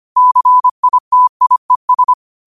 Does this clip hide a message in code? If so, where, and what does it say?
Yes: Morse code from 0.2 to 2.1 s, heard as a keyed tone, about 25 wpm, 1000 Hz. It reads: CITIES